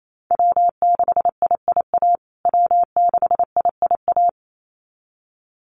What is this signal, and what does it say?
Morse code, heard as a keyed tone, about 28 wpm, 700 Hz. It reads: W6IIA W6IIA
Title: USA Call Signs Repeated 2x 28wpm